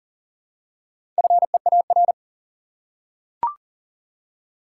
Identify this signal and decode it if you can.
Morse code, heard as a keyed tone, about 40 wpm, 700 Hz. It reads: FEAR
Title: Four Letter Words - 40wpm